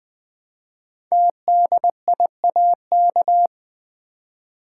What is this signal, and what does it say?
Morse code, heard as a keyed tone, about 20 wpm, 700 Hz. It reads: TDIAK
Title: Code Groups - 5 Letters - 20wpm